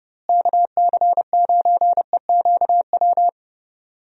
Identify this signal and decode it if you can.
Morse code, heard as a keyed tone, about 30 wpm, 700 Hz. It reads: KC9EQW
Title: USA Call Signs - 6 Characters - Repeated 2x - 30wpm